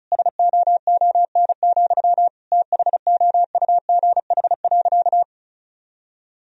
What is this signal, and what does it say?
Morse code, heard as a keyed tone, about 36 wpm, 700 Hz. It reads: SOON, THOUGH.
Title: Sentences from Top 1000 Words - Review Part 2 - 35wpm